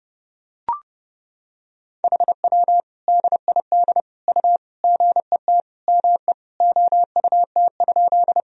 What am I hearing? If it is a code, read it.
Morse code, heard as a keyed tone, about 32 wpm, 700 Hz. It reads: HW DID U GET ME OUT?
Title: Sentences from Top 200 Words and Abbreviations 30wpm